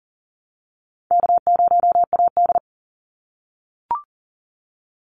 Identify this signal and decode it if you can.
Morse code, heard as a keyed tone, about 40 wpm, 700 Hz. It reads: K0AD
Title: Call Signs - 40wpm